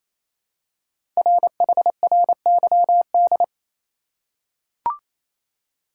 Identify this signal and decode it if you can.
Morse code, heard as a keyed tone, about 27 wpm, 700 Hz. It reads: RHRYD
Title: Code Groups - 5 Letters - 28wpm